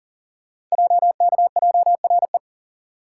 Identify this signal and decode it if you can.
Morse code, heard as a keyed tone, about 42 wpm, 700 Hz. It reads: JKJRE
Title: Code Groups - 5 Letters - 40wpm